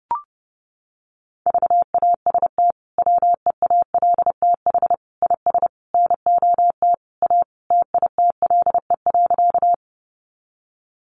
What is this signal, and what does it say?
Morse code, heard as a keyed tone, about 30 wpm, 700 Hz. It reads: VAST WEALTH IS NOT A TITLE.
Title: Sentences from Top 1300 Words 30wpm